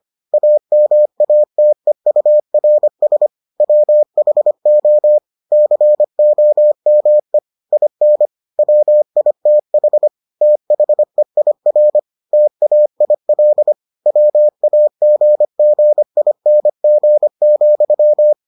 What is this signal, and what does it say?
Morse code, heard as a keyed tone, about 25 wpm, 600 Hz. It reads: AMATEURS WHO COME IN WITH THEIR TAIL WAGGING,